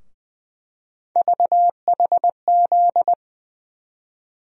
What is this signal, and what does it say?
Morse code, heard as a keyed tone, about 20 wpm, 700 Hz. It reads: VHZ
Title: USA Call Sign Postfixes - 3 Letters 20wpm